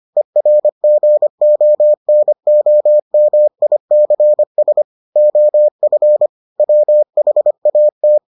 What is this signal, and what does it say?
Morse code, heard as a keyed tone, about 25 wpm, 600 Hz. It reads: ERGONOMICS OF WHAT